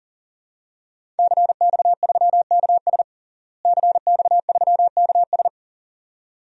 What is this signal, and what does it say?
Morse code, heard as a keyed tone, about 40 wpm, 700 Hz. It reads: CX3KS CX3KS
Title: DXCC-100 Call Signs - 5 Characters - Repeated 2x - 40wpm